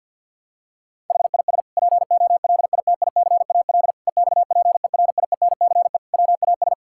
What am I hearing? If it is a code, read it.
Morse code, heard as a keyed tone, about 50 wpm, 700 Hz. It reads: HIS POLITICAL EXPERIENCE WAS
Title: Sentences from Top 1000 Words - Review Part 6 - 50wpm